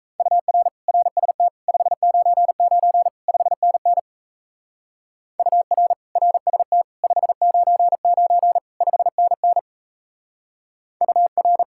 Morse code, heard as a keyed tone, about 34 wpm, 700 Hz. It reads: UR RST 599 5NN UR RST 599 5NN UR
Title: Speed-Racing Common QSO Phrases - 28wpm